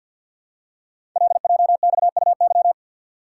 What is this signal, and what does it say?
Morse code, heard as a keyed tone, about 51 wpm, 700 Hz. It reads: LJXUY
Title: Code Groups - 5 Letter:Numbers - 50wpm